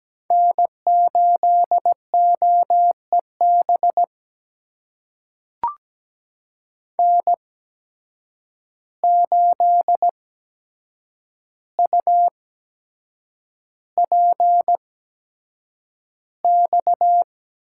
Morse code, heard as a keyed tone, about 17 wpm, spaced as for 7 wpm, 700 Hz. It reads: N8OEB N8UPX
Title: USA Call Signs - 5 Characters - 8x Character Spacing - 17wpm